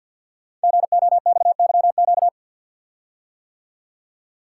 Morse code, heard as a keyed tone, about 46 wpm, 700 Hz. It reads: GOXYQ
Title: Code Groups - 5 Letter:Numbers - 50wpm